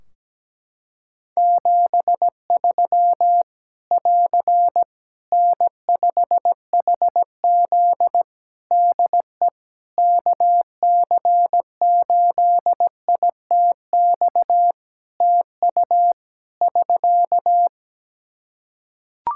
Morse code, heard as a keyed tone, about 17 wpm, 700 Hz. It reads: 73 <AR> N5HZ DE KC8ITX TU <SK>
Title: Practice QSO - Exchange 4 - Ending - 17wpm